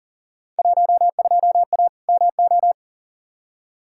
Morse code, heard as a keyed tone, about 40 wpm, 700 Hz. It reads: 12A MO
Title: Field Day Exchanges Encore - 40wpm